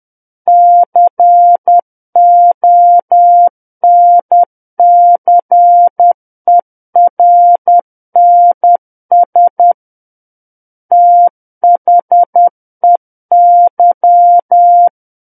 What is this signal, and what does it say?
Morse code, heard as a keyed tone, about 10 wpm, 700 Hz. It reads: CONCERNS THEY
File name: Book One / 01-Chapter 1 - 10 WPM